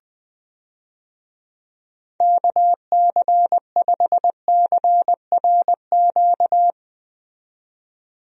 Morse code, heard as a keyed tone, about 20 wpm, 700 Hz. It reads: KC5CRQ
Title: USA Call Signs - 6 Characters - 20wpm